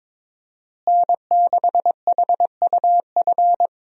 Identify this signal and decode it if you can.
Morse code, heard as a keyed tone, about 22 wpm, 700 Hz. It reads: N6HUF
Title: USA Call Signs - 5 Characters - Repeated 2x - 22wpm